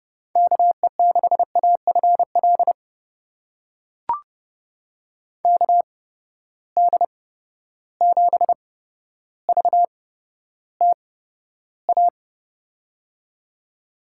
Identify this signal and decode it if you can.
Morse code, heard as a keyed tone, about 30 wpm, spaced as for 13 wpm, 700 Hz. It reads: KE6AFL KD7VTA
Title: USA Call Signs - 6 Characters - 8x Character Spacing - 30wpm